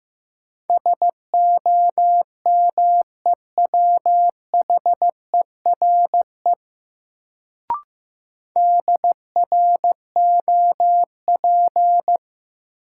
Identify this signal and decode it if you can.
Morse code, heard as a keyed tone, about 15 wpm, 700 Hz. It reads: SOMEWHERE DROP
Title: Sentences from Top 1400 Words - New Word Drill Alt - 15wpm